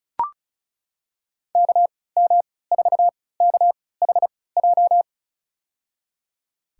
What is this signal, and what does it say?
Morse code, heard as a keyed tone, about 34 wpm, spaced as for 20 wpm, 700 Hz. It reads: KM4KHJ
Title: USA Call Signs - 6 Characters - 3x Character Spacing - 35wpm